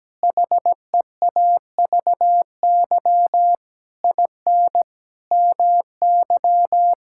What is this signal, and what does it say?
Morse code, heard as a keyed tone, about 17 wpm, 700 Hz. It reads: HEAVY IN MY
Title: Sentences from Top 1000 Words - Review Part 6 - 17wpm